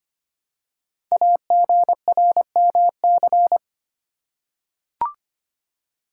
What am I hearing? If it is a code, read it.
Morse code, heard as a keyed tone, about 26 wpm, 700 Hz. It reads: AGRMC
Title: Code Groups - 5 Letters - 25wpm